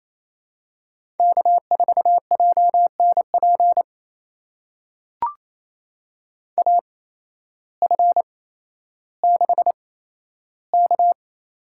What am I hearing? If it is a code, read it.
Morse code, heard as a keyed tone, about 28 wpm, spaced as for 13 wpm, 700 Hz. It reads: K4JNP AF6K
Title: USA Call Signs - 5 Characters - 8x Character Spacing - 28wpm